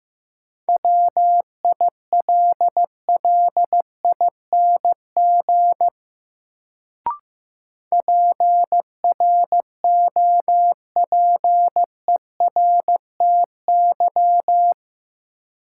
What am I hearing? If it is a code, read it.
Morse code, heard as a keyed tone, about 15 wpm, 700 Hz. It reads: WILLING PROPERTY
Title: Sentences from Top 1000 Words - New Word Drill - 15wpm